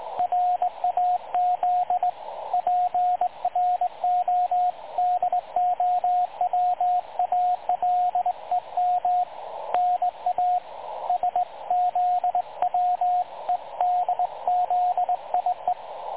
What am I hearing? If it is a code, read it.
Morse code, heard as a keyed tone, about 20 wpm, 720 Hz, with radio noise. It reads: RAZ PRODOWALEM NA SZWEDZIE
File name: dlazaawansowanych / nagraniazpasma / websdr recording 3565.0kHz